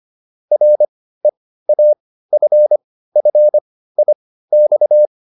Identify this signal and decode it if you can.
Morse code, heard as a keyed tone, about 25 wpm, spaced as for 15 wpm, 600 Hz. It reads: REAFFIX